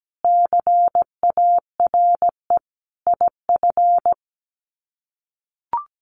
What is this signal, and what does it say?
Morse code, heard as a keyed tone, about 17 wpm, 700 Hz. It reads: CARE IF
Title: Sets of 5 Words 17wpm